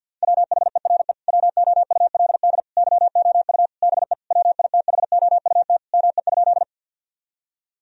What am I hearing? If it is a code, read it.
Morse code, heard as a keyed tone, about 50 wpm, 700 Hz. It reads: WHERE WOULD YOU BE WITHOUT ME?
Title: Sentences from Top 1000 Words - Review Part 3 - 50wpm